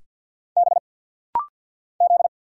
Morse code, heard as a keyed tone, about 50 wpm, 690 Hz. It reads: B Z
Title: Single Letters - Rapid-Fire 50wpm